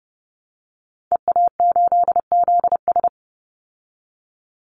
Morse code, heard as a keyed tone, about 30 wpm, 700 Hz. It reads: EA8ZS
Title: Call Signs Repeated 3x - 30wpm